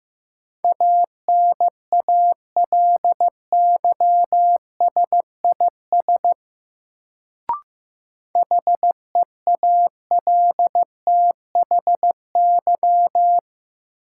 Morse code, heard as a keyed tone, about 15 wpm, 700 Hz. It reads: ANALYSIS HEALTHY
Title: Sentences from Top 1400 Words - New Word Drill Alt - 15wpm